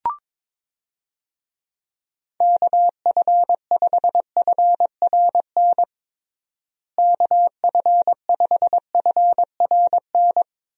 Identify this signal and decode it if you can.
Morse code, heard as a keyed tone, about 22 wpm, 700 Hz. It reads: KF5FRN KF5FRN
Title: USA Call Signs - 6 Characters - Repeated 2x - 22wpm